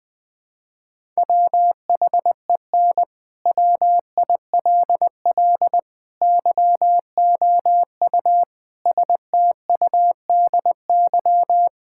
Morse code, heard as a keyed tone, about 21 wpm, 700 Hz. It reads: WHEN WILL YOU STUDY
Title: Sentences from Top 1000 Words - Review Part 6 - 20wpm